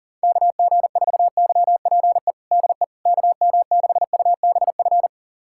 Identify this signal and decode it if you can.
Morse code, heard as a keyed tone, about 40 wpm, 700 Hz. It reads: KG4YPE DE KM6UBF
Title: USA Callsign DE Callsign 40wpm